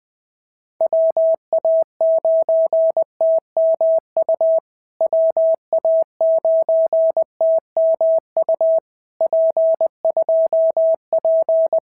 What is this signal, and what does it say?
Morse code, heard as a keyed tone, about 20 wpm, 650 Hz. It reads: WA9TMU WA9TMU P2P
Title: POTA QSO Practice - 20wpm